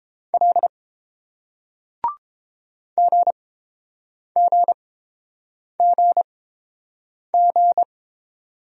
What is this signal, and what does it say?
Morse code, heard as a keyed tone, about 29 wpm, spaced as for 7 wpm, 690 Hz. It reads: L GGGG